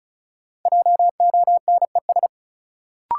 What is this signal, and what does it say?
Morse code, heard as a keyed tone, about 36 wpm, 700 Hz. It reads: JONES
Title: Sets of 5 Words - New Word Drill - 35wpm